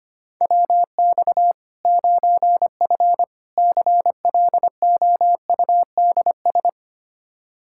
Morse code, heard as a keyed tone, about 25 wpm, 700 Hz. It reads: WX 9F CLOUDS
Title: Practice QSO - Exchange 2 - Info - 25wpm